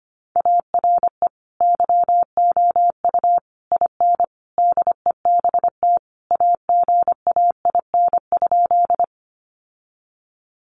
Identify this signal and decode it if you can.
Morse code, heard as a keyed tone, about 25 wpm, 700 Hz. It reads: ARE YOU IN DEBT AGAIN?
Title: Sentences from Top 1300 Words 25wpm